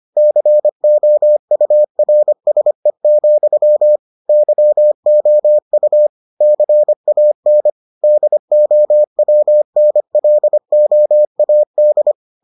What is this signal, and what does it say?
Morse code, heard as a keyed tone, about 25 wpm, 600 Hz. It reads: COURSE, YOU CAN DOWNLOAD